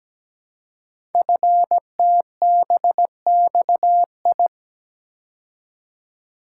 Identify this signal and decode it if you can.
Morse code, heard as a keyed tone, about 17 wpm, 700 Hz. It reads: FTBXI
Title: Code Groups - 5 Letters - 17wpm